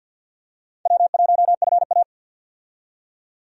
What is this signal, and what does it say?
Morse code, heard as a keyed tone, about 53 wpm, 700 Hz. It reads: W1FA
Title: USA Call Signs - 4 Characters - 50wpm